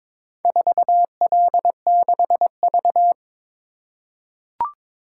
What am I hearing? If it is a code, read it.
Morse code, heard as a keyed tone, about 23 wpm, 700 Hz. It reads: 4L6V
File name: practice / 22wpm / Four Letter-Number Combinations - 22wpm